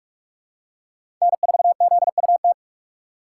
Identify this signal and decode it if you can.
Morse code, heard as a keyed tone, about 44 wpm, 690 Hz. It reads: N4ZUT
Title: USA Call Signs - 5 Characters - 6x Character Spacing - 45wpm